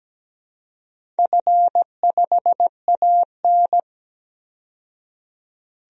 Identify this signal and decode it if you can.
Morse code, heard as a keyed tone, about 17 wpm, 700 Hz. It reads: F5AN
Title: DXCC-100 Call Signs - 4 Characters - 17wpm